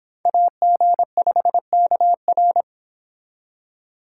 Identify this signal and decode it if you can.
Morse code, heard as a keyed tone, about 30 wpm, 700 Hz. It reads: AG5KR